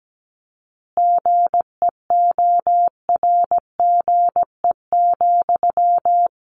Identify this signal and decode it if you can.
Morse code, heard as a keyed tone, about 17 wpm, 700 Hz. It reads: GEORGE,
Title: Additional Easy Phrases 17wpm